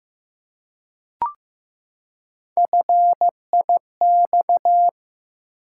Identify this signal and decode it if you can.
Morse code, heard as a keyed tone, about 15 wpm, 700 Hz. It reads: FIX
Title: Three Letter Words - 15wpm